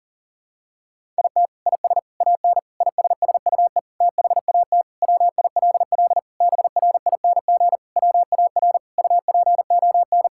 Morse code, heard as a keyed tone, about 40 wpm, 700 Hz. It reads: IT IS AN ISSUE THAT WILL BRING WAR UPON
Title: Sentences from Top 1000 Words - Review Part 6 - 40wpm